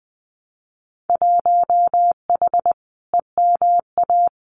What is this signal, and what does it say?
Morse code, heard as a keyed tone, about 20 wpm, 700 Hz. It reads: 1H EMA
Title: Winter Field Day Exchanges 20wpm